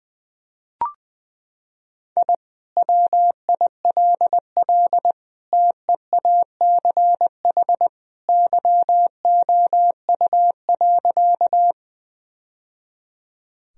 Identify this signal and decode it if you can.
Morse code, heard as a keyed tone, about 20 wpm, 700 Hz. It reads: I WILL TEACH YOU.
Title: Sentences from Top 1000 Words - Review Part 2 - 20wpm